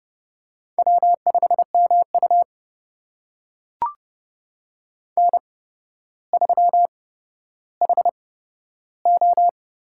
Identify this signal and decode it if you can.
Morse code, heard as a keyed tone, about 30 wpm, spaced as for 13 wpm, 700 Hz. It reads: W5MU N3HO
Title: USA Call Signs - 8x Character Spacing - 30wpm